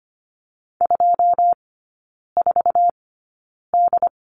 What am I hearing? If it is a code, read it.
Morse code, heard as a keyed tone, about 25 wpm, spaced as for 9 wpm, 700 Hz. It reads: 24D
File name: Lesson 17e - Letter u - Three Characters 25wpm